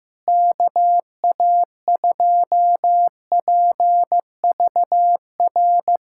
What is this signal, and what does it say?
Morse code, heard as a keyed tone, about 15 wpm, 700 Hz. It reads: KA2PVR
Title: USA Callsign DE Callsign - 6 Characters - 15wpm